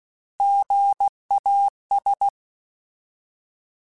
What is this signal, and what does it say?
Morse code, heard as a keyed tone, about 16 wpm, 800 Hz. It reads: GAS